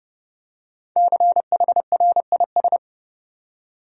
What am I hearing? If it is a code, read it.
Morse code, heard as a keyed tone, about 30 wpm, 700 Hz. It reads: CHRIS
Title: USA Common Names - Max 5 Letters - 30wpm